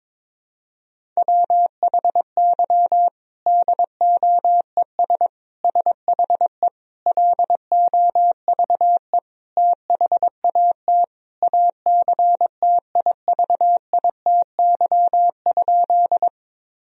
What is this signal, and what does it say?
Morse code, heard as a keyed tone, about 21 wpm, 700 Hz. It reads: WHY DOES SHE LOVE THAT ACTIVITY?
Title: Sentences from Top 1000 Words - Review Part 4 - 22wpm